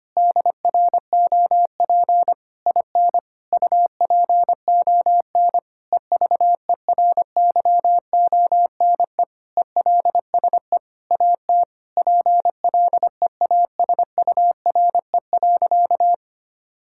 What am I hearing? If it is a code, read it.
Morse code, heard as a keyed tone, about 25 wpm, 700 Hz. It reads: DROP IN UPON EVERYONE ELSE AT PLEASURE.
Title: Sentences from Top 1600 Words 25wpm